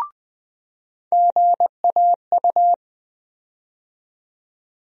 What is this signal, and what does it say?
Morse code, heard as a keyed tone, about 20 wpm, 700 Hz. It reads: GAU
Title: USA Call Sign Postfixes - 3 Letters 20wpm